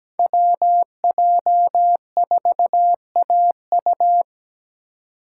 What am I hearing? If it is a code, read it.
Morse code, heard as a keyed tone, about 17 wpm, 700 Hz. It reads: WJ4AU
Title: Code Groups - 5 Letter:Numbers - 17wpm